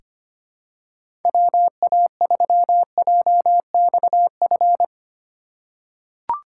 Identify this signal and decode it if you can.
Morse code, heard as a keyed tone, about 25 wpm, 700 Hz. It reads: WA3JXF
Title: USA Call Signs - 6 Characters - 2x Character Spacing - 25wpm